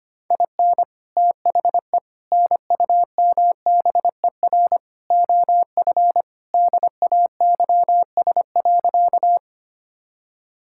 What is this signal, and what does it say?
Morse code, heard as a keyed tone, about 25 wpm, 700 Hz. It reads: IN THE NUMBER OF DAYS.
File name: practice / 25wpm / Sentences with 4-Word Phrases - 25wpm